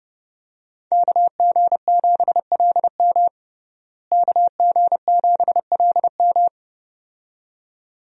Morse code, heard as a keyed tone, about 30 wpm, 700 Hz. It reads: KG7LM KG7LM
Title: USA Call Signs - 5 Characters - Repeated 2x - 30wpm